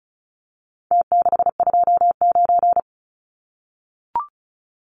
Morse code, heard as a keyed tone, about 34 wpm, 700 Hz. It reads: T629
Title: Numbers - Contest Style - 4-Digit 35wpm